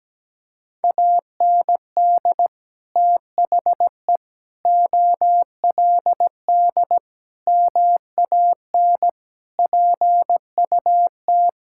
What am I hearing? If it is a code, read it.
Morse code, heard as a keyed tone, about 17 wpm, 700 Hz. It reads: AND THE OLD MAN PUT
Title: Sentences from Top 1600 Words 17wpm